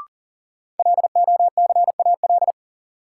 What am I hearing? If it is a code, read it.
Morse code, heard as a keyed tone, about 40 wpm, 700 Hz. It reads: LOCAL
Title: Sentences from Top 1500 Words - New Word Drill Alt - 40wpm